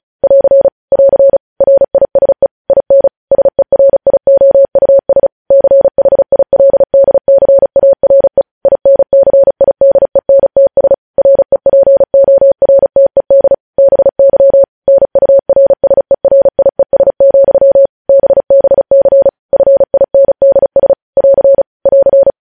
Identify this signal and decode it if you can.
Morse code, heard as a keyed tone, about 36 wpm, 550 Hz. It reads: <AR> <AR> RISE IN SERIOUS CHILDCARE INCIDENTS REPORTED BY NURSERIES, BBC FINDS <AR> <AR>
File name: podcast / News headlines-1x-35x35